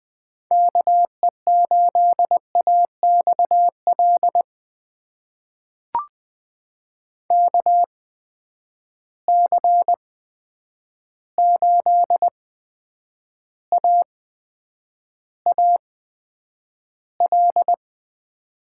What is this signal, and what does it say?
Morse code, heard as a keyed tone, about 20 wpm, spaced as for 9 wpm, 700 Hz. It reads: KE8AXL KC8AAL
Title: USA Call Signs - 6 Characters - 8x Character Spacing - 20wpm